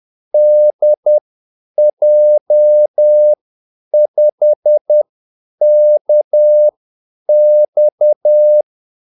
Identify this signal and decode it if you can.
Morse code, heard as a keyed tone, about 10 wpm, spaced as for 8 wpm, 600 Hz. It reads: DJ5KX
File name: Callsigns-10-8wpm